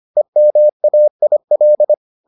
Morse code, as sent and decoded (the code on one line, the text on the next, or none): . -- .- .. .-..
EMAIL